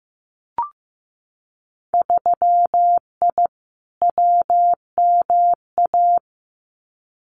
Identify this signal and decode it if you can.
Morse code, heard as a keyed tone, about 15 wpm, 700 Hz. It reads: 3I WMA
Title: Winter Field Day Exchanges 15wpm